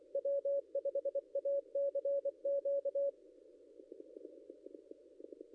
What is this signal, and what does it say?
Morse code, heard as a keyed tone, about 24 wpm, 560 Hz. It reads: W5ACQ
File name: contests / ss-cw-k5tr-op-n5rz -0800